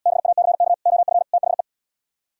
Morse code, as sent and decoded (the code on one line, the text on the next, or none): -... .- -.-. -.- / --- ..-. / - .... .
BACK OF THE